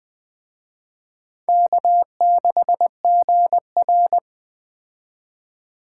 Morse code, heard as a keyed tone, about 20 wpm, 700 Hz. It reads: K6GR
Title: Speed Racing - USA 1x2 Call Signs 17wpm